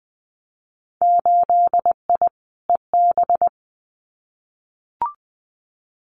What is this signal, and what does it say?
Morse code, heard as a keyed tone, about 20 wpm, 700 Hz. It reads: 8I EB
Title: Winter Field Day Exchanges Encore 20wpm